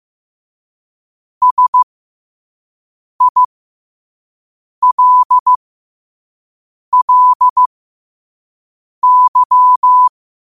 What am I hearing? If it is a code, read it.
Morse code, heard as a keyed tone, about 15 wpm, spaced as for 5 wpm, 1000 Hz. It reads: SILLY
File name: qotdfiles / 5wpm